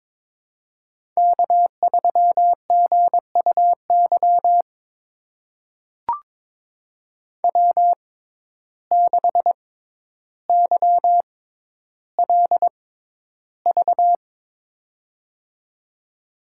Morse code, heard as a keyed tone, about 22 wpm, spaced as for 11 wpm, 700 Hz. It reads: K3GUY W6YLV
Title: USA Call Signs - 5 Characters - 6x Character Spacing - 22wpm